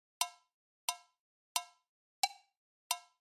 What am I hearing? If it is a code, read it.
There is no code.